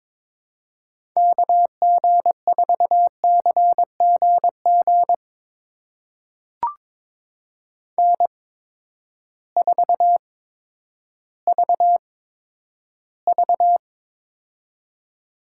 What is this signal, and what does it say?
Morse code, heard as a keyed tone, about 22 wpm, spaced as for 11 wpm, 700 Hz. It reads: KG4CGG N4VV
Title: USA Call Signs - 8x Character Spacing - 22wpm